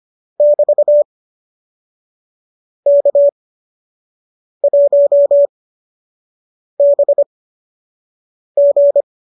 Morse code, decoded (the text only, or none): <BT> K1BG